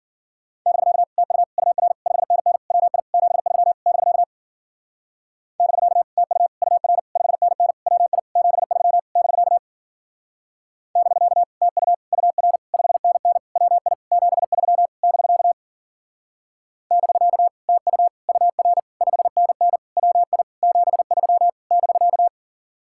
Morse code, heard as a keyed tone, about 50 wpm, 700 Hz. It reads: <BK> TU UR 5NN WI 73 <BK> <BK> TU UR 5NN WI 73 <BK> <BK> TU UR 5NN WI 73 <BK> <BK> TU UR 5NN WI 73 <BK>
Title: Speed-Racing Common QSO Phrases - 40wpm